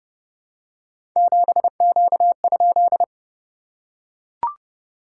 Morse code, heard as a keyed tone, about 30 wpm, 690 Hz. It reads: 7Q?
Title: SOTA QSO Practice 30wpm